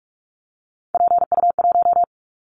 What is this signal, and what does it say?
Morse code, heard as a keyed tone, about 44 wpm, 700 Hz. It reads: PU1
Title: Callsign Prefixes for Top 100 DXCC - 45wpm